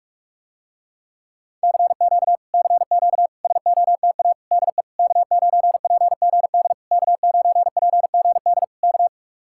Morse code, heard as a keyed tone, about 49 wpm, 700 Hz. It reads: CQ CQ SOTA DE K9PGD K9PGD K